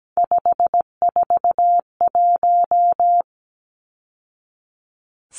Morse code, heard as a keyed tone, about 17 wpm, 700 Hz. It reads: 541
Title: Numbers - 3-Digit 17wpm